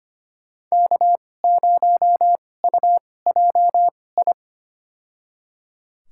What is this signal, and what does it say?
Morse code, heard as a keyed tone, about 25 wpm, spaced as for 18 wpm, 700 Hz. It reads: K0UJI
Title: USA Call Signs - 5 Characters - 2x Character Spacing - 25wpm